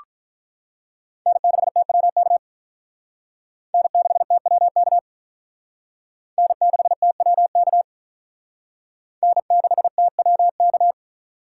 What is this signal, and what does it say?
Morse code, heard as a keyed tone, about 45 wpm, 700 Hz. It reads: N6TWK N6TWK N6TWK N6TWK